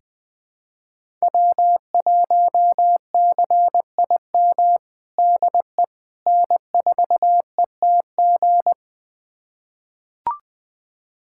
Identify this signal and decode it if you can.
Morse code, heard as a keyed tone, about 20 wpm, 700 Hz. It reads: W1CIM DE N4ETG
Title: USA Callsign DE Callsign - 5 Characters - 20wpm